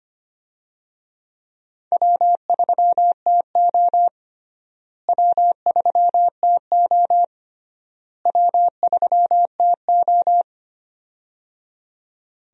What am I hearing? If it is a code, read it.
Morse code, heard as a keyed tone, about 25 wpm, 700 Hz. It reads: W3TO W3TO W3TO
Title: USA Call Signs - 4 Characters - Repeated 3x - 25wpm